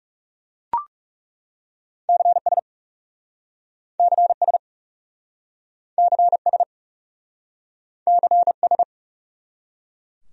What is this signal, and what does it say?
Morse code, heard as a keyed tone, about 38 wpm, 700 Hz. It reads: CS CS CS CS